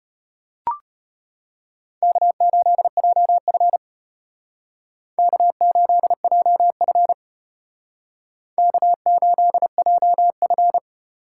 Morse code, heard as a keyed tone, about 33 wpm, 700 Hz. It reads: K8JF K8JF K8JF